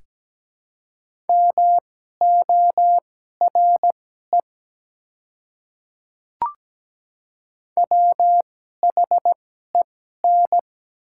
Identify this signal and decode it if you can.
Morse code, heard as a keyed tone, about 17 wpm, spaced as for 12 wpm, 700 Hz. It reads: MORE WHEN
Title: 2x Character Spacing - Top 200 Words - 17wpm